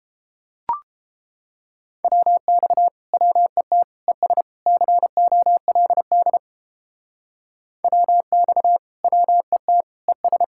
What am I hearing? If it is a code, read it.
Morse code, heard as a keyed tone, about 31 wpm, 700 Hz. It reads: WX WET ES COLD WX WET ES